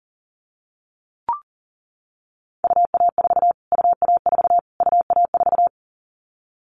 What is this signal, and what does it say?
Morse code, heard as a keyed tone, about 40 wpm, 700 Hz. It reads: UA4 UA4 UA4
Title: Callsign Prefixes for Top 25 DXCC - Repeated 3x - 40wpm